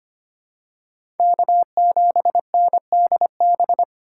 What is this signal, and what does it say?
Morse code, heard as a keyed tone, about 25 wpm, 700 Hz. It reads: K7NDB
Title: USA Call Signs - 5 Characters - 2x Character Spacing - 25wpm